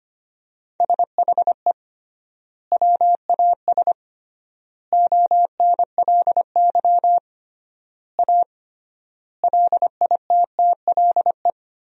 Morse code, heard as a keyed tone, about 25 wpm, 700 Hz. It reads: SHE WAS ONLY A LITTLE
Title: 3x Word-Spacing - Sentences from Top 400 Words - 25wpm